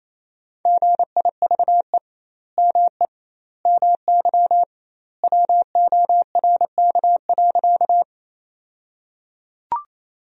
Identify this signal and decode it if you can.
Morse code, heard as a keyed tone, about 28 wpm, 700 Hz. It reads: GIVE ME MY WORK.
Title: 2x Word-Spacing - Sentences from Top 200 Words - 28wpm